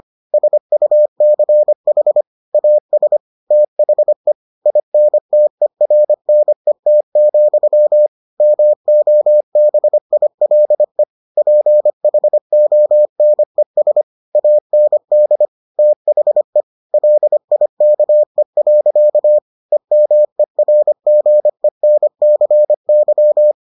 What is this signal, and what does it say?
Morse code, heard as a keyed tone, about 25 wpm, 600 Hz. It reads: SUCH AS THE INTERNET, MOBILE PHONES AND THE LIKE. EMERGENCY